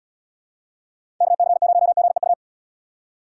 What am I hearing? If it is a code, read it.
Morse code, heard as a keyed tone, about 78 wpm, 700 Hz. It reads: KC0GEU